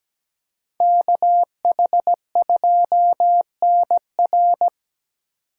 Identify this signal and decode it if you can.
Morse code, heard as a keyed tone, about 20 wpm, 700 Hz. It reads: KH2NR
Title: Speed Racing - USA 2x2 Call Signs 17wpm